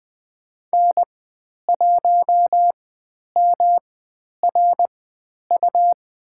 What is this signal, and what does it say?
Morse code, heard as a keyed tone, about 20 wpm, spaced as for 10 wpm, 700 Hz. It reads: N1MRU
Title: 109 Lesson 18g - Letter m - Call Signs 20wpm